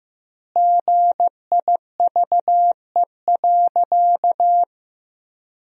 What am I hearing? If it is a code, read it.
Morse code, heard as a keyed tone, about 15 wpm, 700 Hz. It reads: GIVE.
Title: 2x Word-Spacing - Sentences from Top 200 Words - 15wpm